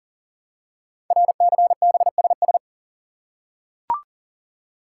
Morse code, heard as a keyed tone, about 39 wpm, 700 Hz. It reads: RCBSS
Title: Code Groups - 5 Letters - 40wpm